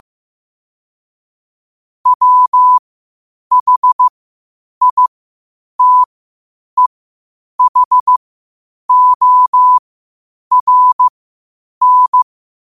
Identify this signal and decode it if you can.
Morse code, heard as a keyed tone, about 15 wpm, spaced as for 8 wpm, 1000 Hz. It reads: WHITEHORN
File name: 8wpm